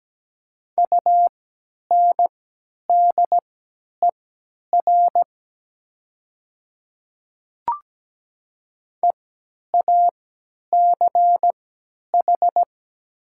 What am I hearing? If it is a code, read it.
Morse code, heard as a keyed tone, about 17 wpm, spaced as for 10 wpm, 700 Hz. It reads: UNDER EACH
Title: 3x Character Spacing - Top 200 Words - 17wpm